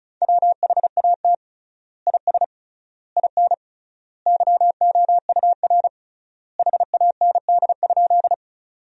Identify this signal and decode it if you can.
Morse code, heard as a keyed tone, about 35 wpm, 700 Hz. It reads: WHAT IS IN YOUR HAND?
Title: 3x Word-Spacing - Sentences from Top 200 Words - 35wpm